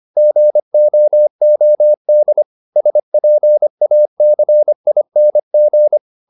Morse code, heard as a keyed tone, about 25 wpm, 600 Hz. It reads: GOOD SPACING